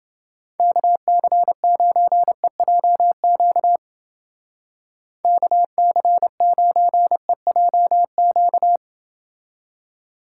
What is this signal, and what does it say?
Morse code, heard as a keyed tone, about 29 wpm, 700 Hz. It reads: KC9EJQ KC9EJQ